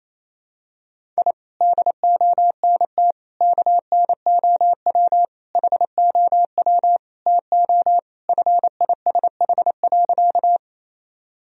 Morse code, heard as a keyed tone, about 28 wpm, 700 Hz. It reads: I DONT KNOW HOW TO FISH.
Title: Sentences from Top 1000 Words - Review Part 1 - 28wpm